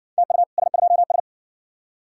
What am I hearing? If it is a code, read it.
Morse code, heard as a keyed tone, about 55 wpm, 700 Hz. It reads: TU S2S